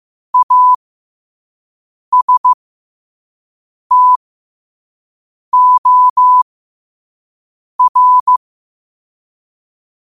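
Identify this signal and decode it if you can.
Morse code, heard as a keyed tone, about 15 wpm, spaced as for 5 wpm, 1000 Hz. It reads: ASTOR
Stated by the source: Morse quote of the day at 5 wpm.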